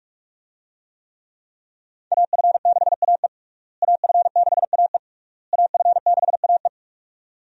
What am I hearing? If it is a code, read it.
Morse code, heard as a keyed tone, about 46 wpm, 700 Hz. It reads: AF6AE AF6AE AF6AE
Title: USA Call Signs - 5 Characters - Repeated 3x - 45wpm